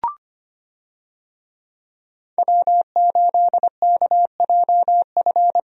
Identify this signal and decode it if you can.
Morse code, heard as a keyed tone, about 25 wpm, 700 Hz. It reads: W8KJF
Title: USA Call Signs - 5 Characters - Repeated 3x - 25wpm